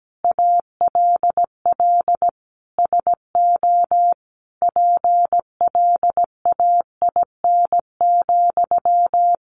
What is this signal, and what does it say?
Morse code, heard as a keyed tone, about 17 wpm, 700 Hz. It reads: ALL SO PLAIN,
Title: Sentences from Top 600 Words 17wpm